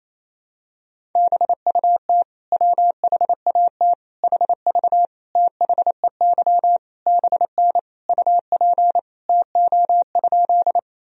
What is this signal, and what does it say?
Morse code, heard as a keyed tone, about 28 wpm, 700 Hz. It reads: BUT WHAT HV THEY BN UP TO?
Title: Sentences from Top 100 Words and Abbreviations 28wpm